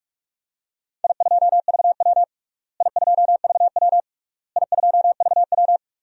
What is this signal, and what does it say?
Morse code, heard as a keyed tone, about 47 wpm, 700 Hz. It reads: I2VW I2VW I2VW
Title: DXCC-100 Call Signs - 4 Characters - Repeated 3x - 45wpm